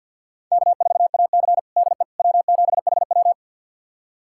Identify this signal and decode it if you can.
Morse code, heard as a keyed tone, about 47 wpm, 700 Hz. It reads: K4AC DE W7HW
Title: USA Callsign DE Callsign - 4 Characters - 50wpm